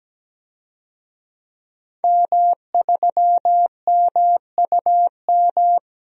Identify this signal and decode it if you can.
Morse code, heard as a keyed tone, about 17 wpm, 700 Hz. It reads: M3MUM
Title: DXCC-100 Call Signs - 5 Characters - Repeated 2x - 17wpm